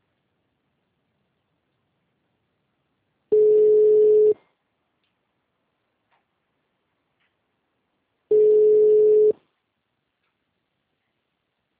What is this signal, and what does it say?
Ringback tone (Europe, 425 Hz).